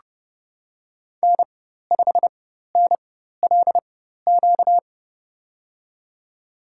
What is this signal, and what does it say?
Morse code, heard as a keyed tone, about 30 wpm, spaced as for 14 wpm, 700 Hz. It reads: N5NLQ
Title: USA Call Signs - 5 Characters - 4x Character Spacing - 30wpm